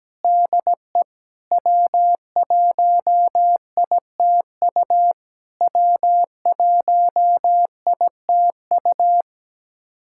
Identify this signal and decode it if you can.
Morse code, heard as a keyed tone, about 17 wpm, 700 Hz. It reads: DE W1ITU W1ITU
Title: SOTA QSO Practice 17wpm